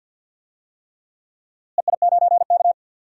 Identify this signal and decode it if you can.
Morse code, heard as a keyed tone, about 49 wpm, 700 Hz. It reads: EI9K
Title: DXCC-100 Call Signs - 4 Characters - Repeated 3x - 50wpm